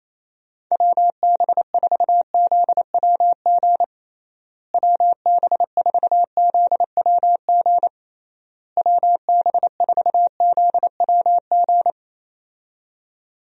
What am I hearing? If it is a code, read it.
Morse code, heard as a keyed tone, about 28 wpm, 700 Hz. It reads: WB4ZWG WB4ZWG WB4ZWG
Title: USA Call Signs - 6 Characters - Repeated 3x - 28wpm